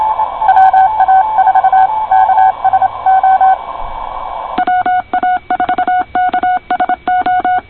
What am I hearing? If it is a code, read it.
Morse code, heard as a keyed tone, about 25 wpm, 720 Hz, with radio noise. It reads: WA4KSO WA4KSO
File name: co8tw / cqwwcw2010